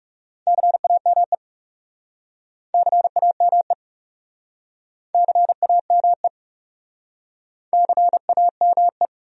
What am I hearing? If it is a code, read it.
Morse code, heard as a keyed tone, about 37 wpm, 700 Hz. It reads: CAME CAME CAME CAME
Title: Speed-Racing Top 500 Words 30wpm